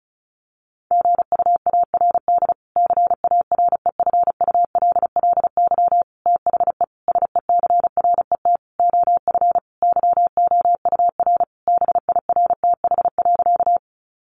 Morse code, heard as a keyed tone, about 34 wpm, 700 Hz. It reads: GUARD CAREFULLY THE SECRET OF YOUR BIRTH.
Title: Sentences from Top 1300 Words 35wpm